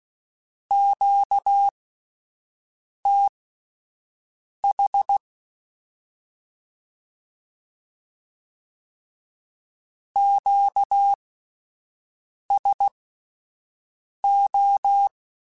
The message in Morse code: QTH QSO

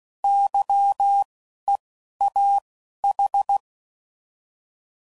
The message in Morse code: YEAH

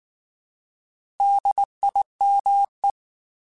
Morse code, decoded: DIME

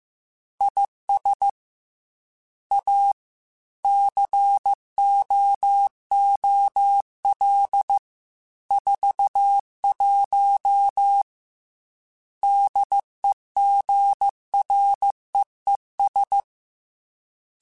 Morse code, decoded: IS A COOL 41 DEGREES

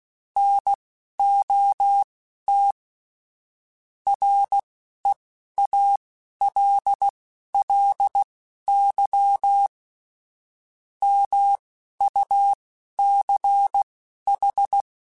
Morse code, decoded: NOT REALLY MUCH

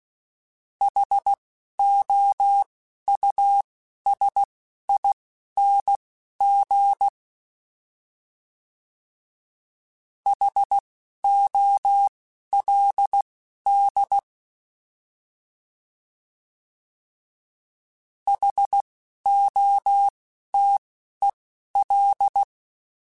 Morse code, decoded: HOUSING HOLD HOTEL

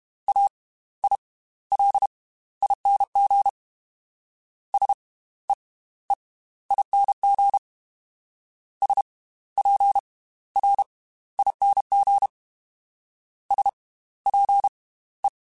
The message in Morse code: AILING SEEING SPRING SPE

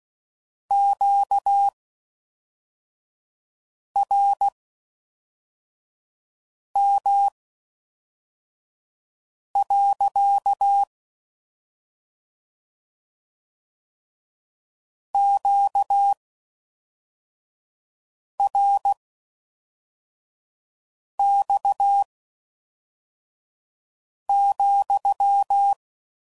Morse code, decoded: QRM. QRX,

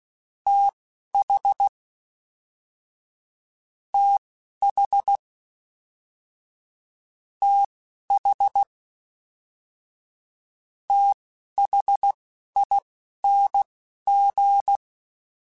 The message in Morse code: TH TH TH THING

